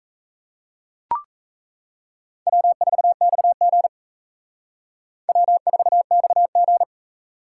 Morse code, heard as a keyed tone, about 36 wpm, 700 Hz. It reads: W4XG W4XG